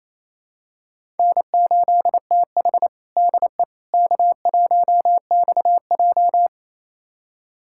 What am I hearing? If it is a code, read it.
Morse code, heard as a keyed tone, about 28 wpm, 700 Hz. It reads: N8TH DE K1XJ
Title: USA Callsign DE Callsign - 4 Characters - 28wpm